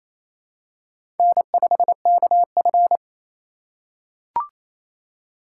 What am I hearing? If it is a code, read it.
Morse code, heard as a keyed tone, about 28 wpm, 700 Hz. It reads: N5KF
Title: USA Call Signs - 4 Characters - 2x Character Spacing - 28wpm